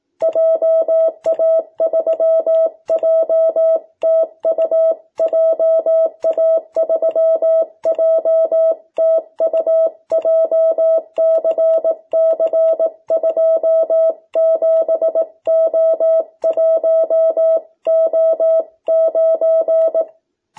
Morse code, heard as a keyed tone, about 18 wpm, 660 Hz. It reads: JA3JTU JA3JTU JCC27O1O9
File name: jh8cbh / 2021cw